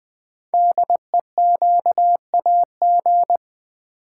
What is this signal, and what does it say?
Morse code, heard as a keyed tone, about 20 wpm, 700 Hz. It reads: DEQAG
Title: Code Groups - 5 Letters - 20wpm